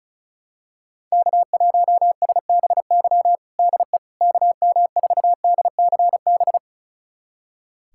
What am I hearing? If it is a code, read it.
Morse code, heard as a keyed tone, about 36 wpm, 700 Hz. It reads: K1SBY DE KM4DCB
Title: USA Callsign DE Callsign 35wpm